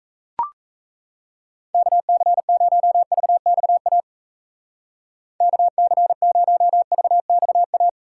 Morse code, heard as a keyed tone, about 36 wpm, 700 Hz. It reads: KC0VXA KC0VXA